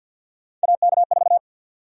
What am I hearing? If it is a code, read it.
Morse code, heard as a keyed tone, about 53 wpm, 700 Hz. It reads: AK4
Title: USA Call Sign Prefixes - Repeated 2x - 50wpm